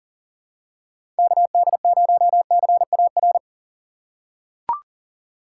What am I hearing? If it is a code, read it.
Morse code, heard as a keyed tone, about 40 wpm, 700 Hz. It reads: KD0CAR
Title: USA Call Signs - 4x Character Spacing - 40wpm